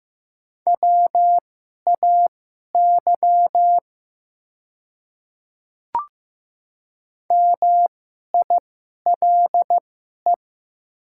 Morse code, heard as a keyed tone, about 15 wpm, spaced as for 11 wpm, 700 Hz. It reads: WAY MILE
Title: 2x Character Spacing - Top 300 Words - 15wpm